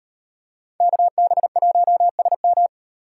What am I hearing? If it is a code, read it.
Morse code, heard as a keyed tone, about 38 wpm, 700 Hz. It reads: KB1SM